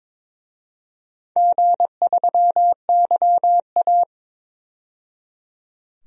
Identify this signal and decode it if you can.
Morse code, heard as a keyed tone, about 22 wpm, 700 Hz. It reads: G3YA
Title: DXCC-100 Call Signs - 4 Characters - 22wpm